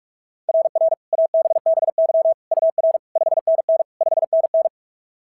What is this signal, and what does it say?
Morse code, heard as a keyed tone, about 46 wpm, 690 Hz. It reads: RR ABBY UR 5NN 5NN
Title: SOTA QSO Practice 45wpm